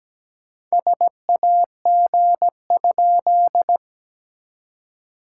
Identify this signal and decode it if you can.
Morse code, heard as a keyed tone, about 17 wpm, 700 Hz. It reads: SAG?
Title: SOTA QSO Practice 17wpm